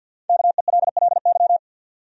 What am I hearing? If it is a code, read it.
Morse code, heard as a keyed tone, about 51 wpm, 700 Hz. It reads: KELLY
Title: USA Common Names - Max 5 Letters - 50wpm